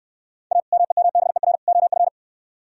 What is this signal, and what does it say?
Morse code, heard as a keyed tone, about 64 wpm, 700 Hz. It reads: A MEMBER OF